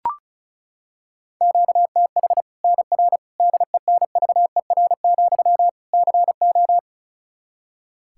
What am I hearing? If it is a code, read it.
Morse code, heard as a keyed tone, about 34 wpm, 700 Hz. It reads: QTH NR DENVER, CO
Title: K1USN SST QSO Practice 35wpm